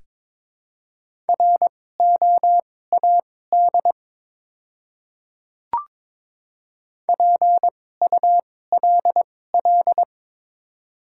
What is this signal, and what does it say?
Morse code, heard as a keyed tone, about 22 wpm, spaced as for 16 wpm, 700 Hz. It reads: ROAD PULL
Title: 2x Character Spacing - Top 500 Words - 22wpm